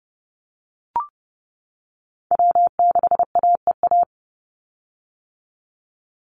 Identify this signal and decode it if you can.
Morse code, heard as a keyed tone, about 30 wpm, 700 Hz. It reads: W6AEA
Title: Call Signs - 30wpm